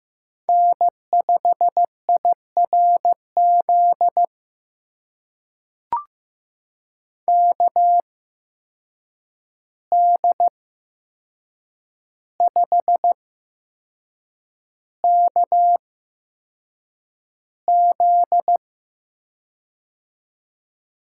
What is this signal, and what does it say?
Morse code, heard as a keyed tone, about 15 wpm, spaced as for 6 wpm, 700 Hz. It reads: N5IRZ KD5KZ
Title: USA Call Signs - 5 Characters - 8x Character Spacing - 15wpm